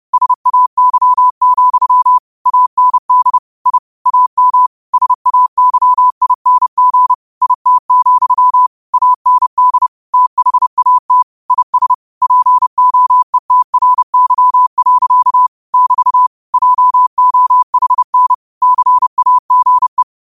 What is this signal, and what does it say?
Morse code, heard as a keyed tone, about 30 wpm, 1000 Hz. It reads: SAY, AND I AM SAYING IT, AND THAT IS POETRY. <BT> JOHN CAGE
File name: qotdfiles / 30wpm